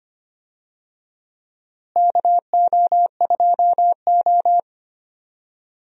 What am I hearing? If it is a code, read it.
Morse code, heard as a keyed tone, about 25 wpm, 700 Hz. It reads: KO2O
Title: USA Call Signs - 4 Characters - Repeated 3x - 25wpm